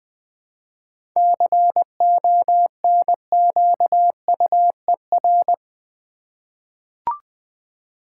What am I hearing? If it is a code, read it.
Morse code, heard as a keyed tone, about 20 wpm, 700 Hz. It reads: CONQUER
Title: 7 Letter Words - First 3 Letters Given - 20wpm